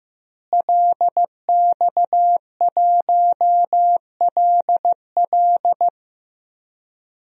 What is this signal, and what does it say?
Morse code, heard as a keyed tone, about 15 wpm, 700 Hz. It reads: LX1LL
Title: DXCC-100 Call Signs - 5 Characters - Repeated 2x - 15wpm